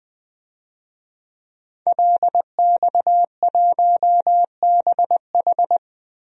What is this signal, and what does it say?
Morse code, heard as a keyed tone, about 20 wpm, 700 Hz. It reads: LX1BH
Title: DXCC-100 Call Signs - 5 Characters - 20wpm